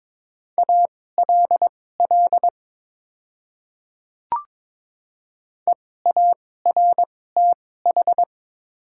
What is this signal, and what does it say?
Morse code, heard as a keyed tone, about 22 wpm, spaced as for 16 wpm, 700 Hz. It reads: ALL EARTH
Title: 2x Character Spacing - Top 200 Words - 22wpm